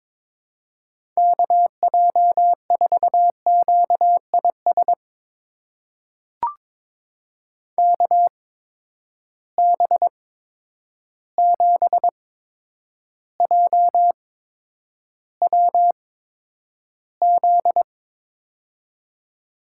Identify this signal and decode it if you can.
Morse code, heard as a keyed tone, about 22 wpm, spaced as for 9 wpm, 700 Hz. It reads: KJ4QIS KB7JWZ
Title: USA Call Signs - 6 Characters - 8x Character Spacing - 22wpm